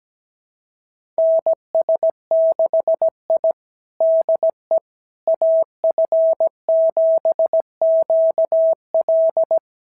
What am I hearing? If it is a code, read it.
Morse code, heard as a keyed tone, about 17 wpm, 650 Hz. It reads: NS6I DE AF7QL
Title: SOTA QSO Practice 17wpm